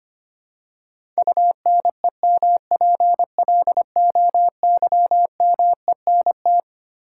Morse code, heard as a keyed tone, about 25 wpm, 700 Hz. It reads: UNEMPLOYMENT
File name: Twelve Letter Words - 25wpm